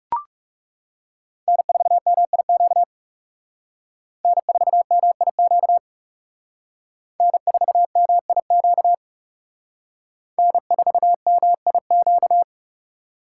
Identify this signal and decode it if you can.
Morse code, heard as a keyed tone, about 37 wpm, 700 Hz. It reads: N4MIQ N4MIQ N4MIQ N4MIQ
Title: Speed Racing - USA 1x3 Call Signs 30wpm